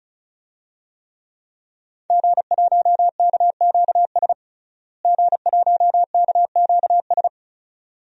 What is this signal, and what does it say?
Morse code, heard as a keyed tone, about 34 wpm, 700 Hz. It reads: G1KQS G1KQS
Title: DXCC-100 Call Signs - 5 Characters - Repeated 2x - 35wpm